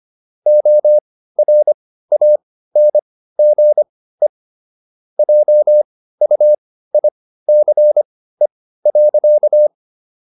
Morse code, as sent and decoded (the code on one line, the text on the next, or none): --- .-. .- -. --. / .--- ..- .. -.-. .-.-.-
ORANGE JUICE.